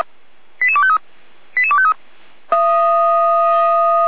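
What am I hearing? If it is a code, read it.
Five-tone selective call (ZVEI), heard as quick runs of five tones, tones of about 70 ms. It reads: 88253 88253
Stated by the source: ZVEI five-tone selective call received on an SDR.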